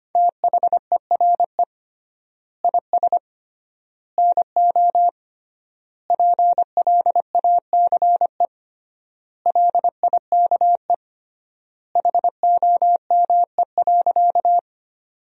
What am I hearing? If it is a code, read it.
Morse code, heard as a keyed tone, about 25 wpm, 700 Hz. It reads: THERE IS NO PLACE LIKE HOME.
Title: 3x Word-Spacing - Sentences from Top 200 Words - 25wpm